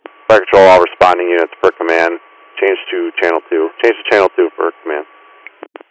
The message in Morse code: none

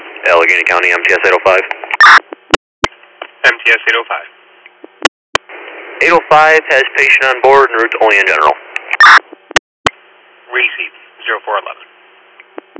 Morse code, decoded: none

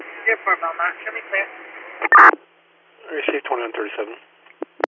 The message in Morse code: none

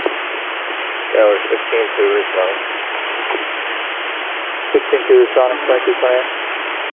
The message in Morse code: none

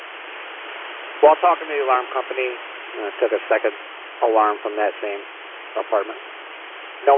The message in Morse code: none

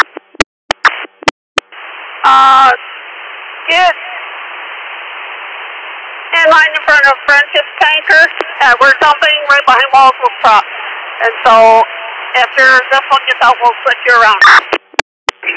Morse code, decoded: none

